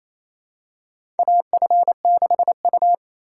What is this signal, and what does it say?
Morse code, heard as a keyed tone, about 28 wpm, 700 Hz. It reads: AF6U
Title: USA Call Signs - 4 Characters - 2x Character Spacing - 28wpm